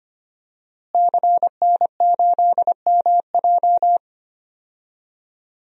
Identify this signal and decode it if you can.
Morse code, heard as a keyed tone, about 25 wpm, 700 Hz. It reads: CN8MJ
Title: DXCC-100 Call Signs - 5 Characters - Repeated 2x - 25wpm